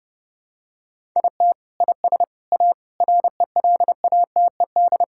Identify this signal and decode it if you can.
Morse code, heard as a keyed tone, about 29 wpm, 700 Hz. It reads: IT IS A RELATED
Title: Sentences from Top 1000 Words - Review Part 5 - 30wpm